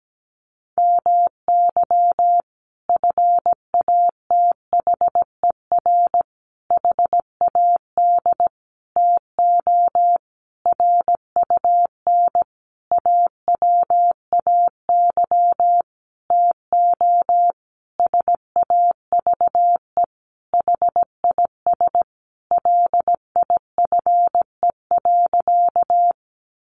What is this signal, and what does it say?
Morse code, heard as a keyed tone, about 17 wpm, 700 Hz. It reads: MY FATHER HAD TO RUN AWAY TO SAVE HIS LIFE.
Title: Sentences from Top 1100 Words 17wpm